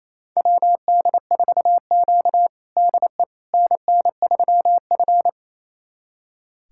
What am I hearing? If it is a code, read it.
Morse code, heard as a keyed tone, about 28 wpm, 690 Hz. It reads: WD4Q DE NN3F
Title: USA Callsign DE Callsign - 4 Characters - 28wpm